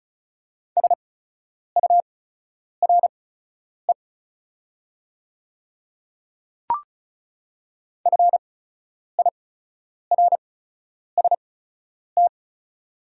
Morse code, heard as a keyed tone, about 34 wpm, spaced as for 10 wpm, 690 Hz. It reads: SURE FIRST
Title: 8x Character Spacing - Top 500 Words - 35wpm